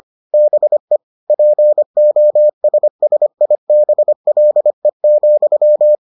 Morse code, heard as a keyed tone, about 25 wpm, 600 Hz. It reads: BE POSSIBLE,